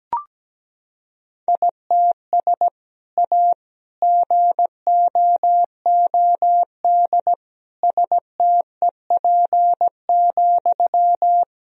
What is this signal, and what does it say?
Morse code, heard as a keyed tone, about 17 wpm, 700 Hz. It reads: ITS A GOOD STEP,
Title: Sentences from Top 1000 Words - Review Part 2 - 17wpm